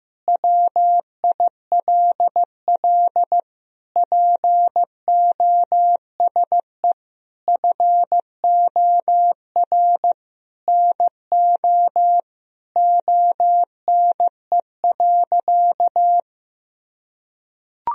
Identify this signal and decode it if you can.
Morse code, heard as a keyed tone, about 15 wpm, 700 Hz. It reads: WILL POSE FOR NO ONE.
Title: Sentences from Top 1000 Words - Review Part 2 - 15wpm